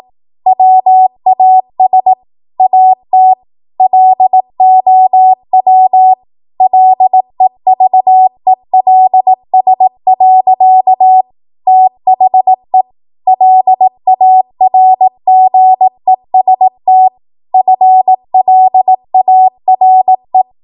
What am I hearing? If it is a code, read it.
Morse code, heard as a keyed tone, about 18 wpm, 750 Hz. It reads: WAS AT LOW LEVELS. THE LARGEST FLARE
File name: Morse / 18WPM